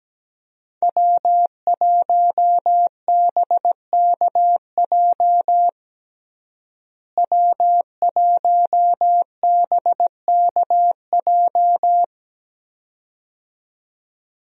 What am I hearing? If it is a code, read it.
Morse code, heard as a keyed tone, about 17 wpm, 700 Hz. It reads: W1BKJ W1BKJ
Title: USA Call Signs - 5 Characters - Repeated 2x - 17wpm